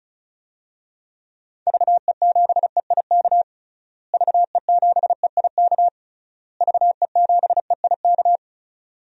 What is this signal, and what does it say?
Morse code, heard as a keyed tone, about 36 wpm, 700 Hz. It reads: VE7EIK VE7EIK VE7EIK
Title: DXCC-100 Call Signs - 6 Characters - Repeated 3x - 35wpm